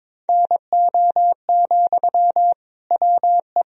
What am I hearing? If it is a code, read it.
Morse code, heard as a keyed tone, about 21 wpm, 700 Hz. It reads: NO, WE
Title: Sentences from Top 1000 Words - Review Part 3 - 22wpm